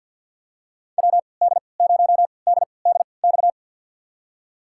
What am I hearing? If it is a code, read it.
Morse code, heard as a keyed tone, about 50 wpm, spaced as for 28 wpm, 700 Hz. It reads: WD0DDX
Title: USA Call Signs - 6 Characters - 3x Character Spacing - 50wpm